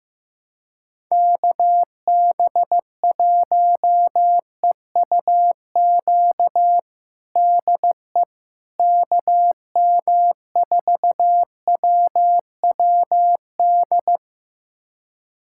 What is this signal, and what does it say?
Morse code, heard as a keyed tone, about 16 wpm, 700 Hz. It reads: KB1EUQ DE KM4WWD
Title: USA Callsign DE Callsign - 6 Characters - 15wpm